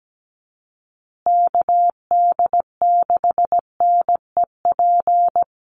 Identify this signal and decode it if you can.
Morse code, heard as a keyed tone, about 17 wpm, 700 Hz. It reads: KD6NEP
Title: USA Call Signs Repeated 3x 17wpm